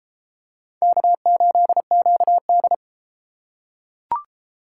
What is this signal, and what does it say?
Morse code, heard as a keyed tone, about 31 wpm, 700 Hz. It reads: K8QD